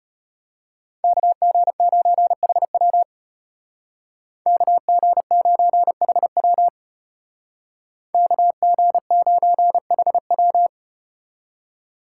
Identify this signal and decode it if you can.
Morse code, heard as a keyed tone, about 35 wpm, 700 Hz. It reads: KG9HW KG9HW KG9HW